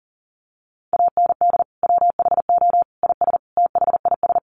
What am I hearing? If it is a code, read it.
Morse code, heard as a keyed tone, about 40 wpm, 700 Hz. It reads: AND WHO IS THIS
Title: Sentences from Top 300 Words 40wpm